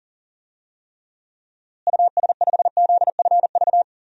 Morse code, heard as a keyed tone, about 40 wpm, 700 Hz. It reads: US5ZFV
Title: DXCC-100 Call Signs - 6 Characters - Repeated 3x - 40wpm